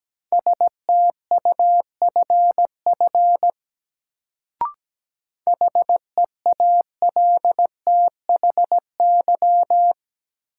Morse code, heard as a keyed tone, about 17 wpm, 700 Hz. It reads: STUFF HEALTHY
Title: Sentences from Top 1400 Words - New Word Drill Alt - 17wpm